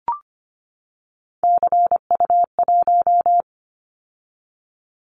Morse code, heard as a keyed tone, about 25 wpm, 700 Hz. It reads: CU1
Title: Callsign Prefixes for Top 50 DXCC - 25wpm